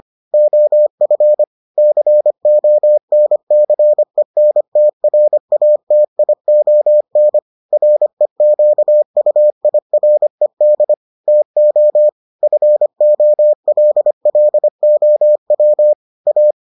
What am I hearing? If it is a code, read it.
Morse code, heard as a keyed tone, about 25 wpm, 600 Hz. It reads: OF CONCENTRATION REQUIRED TO FOLLOW A